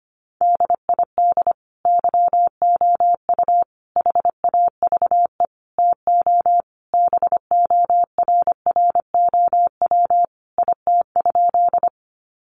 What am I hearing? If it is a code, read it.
Morse code, heard as a keyed tone, about 25 wpm, 700 Hz. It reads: DID YOU HAVE TO BORROW IT?
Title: Sentences from Top 1500 Words 25wpm